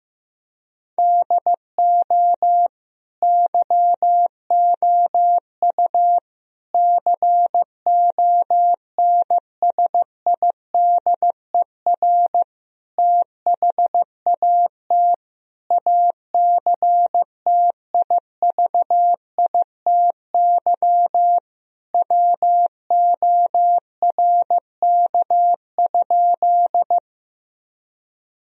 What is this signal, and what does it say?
Morse code, heard as a keyed tone, about 15 wpm, 700 Hz. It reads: DO YOU CONSIDER THAT ACTIVITY WORK?
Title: Sentences from Top 1000 Words - Review Part 4 - 15wpm